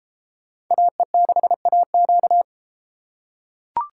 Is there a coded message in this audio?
Yes: Morse code from 0.7 to 2.4 s, heard as a keyed tone, about 32 wpm, 700 Hz. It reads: AE6AQ